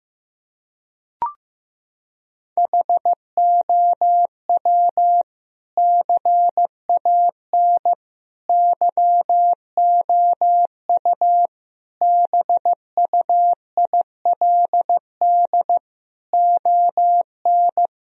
Morse code, heard as a keyed tone, about 15 wpm, 700 Hz. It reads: HOW CAN YOU BUILD ON
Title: Sentences from Top 1000 Words - Review Part 1 - 15wpm